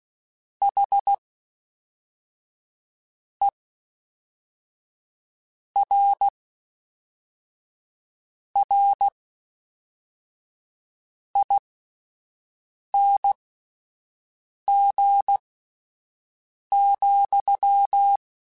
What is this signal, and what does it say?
Morse code, heard as a keyed tone, about 16 wpm, spaced as for 4 wpm, 800 Hz. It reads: HERRING,
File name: Misc / Misc-mp3s / WordPairs sp @5